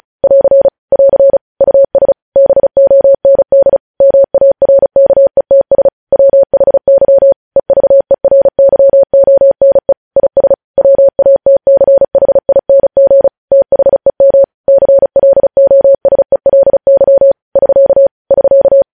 Morse code, heard as a keyed tone, about 36 wpm, 550 Hz. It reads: <AR> <AR> US BOND MARKETS WHY EVERYONE IS WATCHING THEM CLOSELY <SK> <SK>
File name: podcast / News headlines-1x-35x35